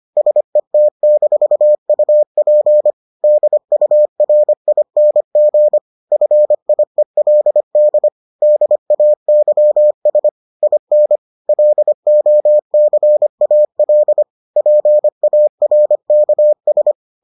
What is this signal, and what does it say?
Morse code, heard as a keyed tone, about 25 wpm, 600 Hz. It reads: SET-UP DURING FIELD DAYS IN LOCAL PARKS